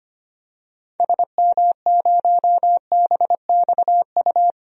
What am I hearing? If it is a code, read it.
Morse code, heard as a keyed tone, about 25 wpm, 700 Hz. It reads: SM0BXU
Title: DXCC-100 Call Signs - 6 Characters - 25wpm